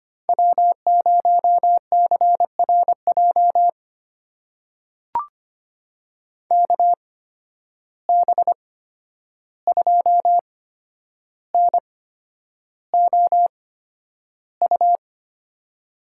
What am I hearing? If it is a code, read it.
Morse code, heard as a keyed tone, about 25 wpm, spaced as for 10 wpm, 700 Hz. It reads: W0CRJ KB2NOU
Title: USA Call Signs - 8x Character Spacing - 25wpm